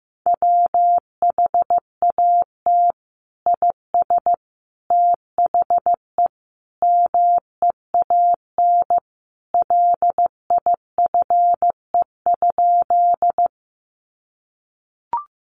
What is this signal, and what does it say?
Morse code, heard as a keyed tone, about 15 wpm, 700 Hz. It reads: WHAT IS THE MEAN LIFE?
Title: Sentences from Top 900 Words 15wpm